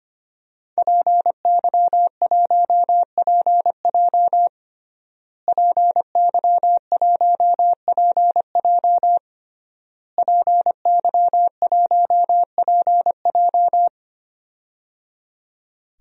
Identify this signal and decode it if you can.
Morse code, heard as a keyed tone, about 25 wpm, 700 Hz. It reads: PY1PJ PY1PJ PY1PJ
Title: DXCC-100 Call Signs - 5 Characters - Repeated 3x - 25wpm